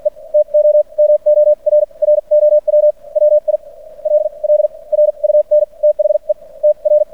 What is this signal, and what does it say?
Morse code, heard as a keyed tone, about 56 wpm, spaced as for 37 wpm, 600 Hz. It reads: E TOMORROW WILL RUN THE TR
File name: archived / CRSnet